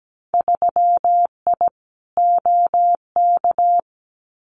Morse code, heard as a keyed tone, about 17 wpm, 700 Hz. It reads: 3I OK
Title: Winter Field Day Exchanges Encore 17wpm